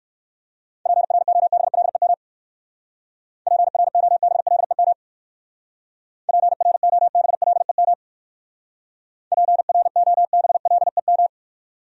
Morse code, heard as a keyed tone, about 54 wpm, 700 Hz. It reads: PROBLEM PROBLEM PROBLEM PROBLEM
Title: Speed-Racing Top 500 Words 45wpm